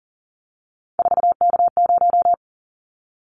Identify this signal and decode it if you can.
Morse code, heard as a keyed tone, about 40 wpm, 700 Hz. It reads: 4K0
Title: Callsign Prefixes for Top 100 DXCC - 40wpm